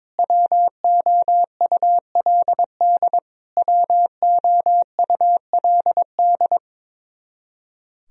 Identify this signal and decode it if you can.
Morse code, heard as a keyed tone, about 22 wpm, 700 Hz. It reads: WOULD WOULD
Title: Top 100 Words Repeated 22wpm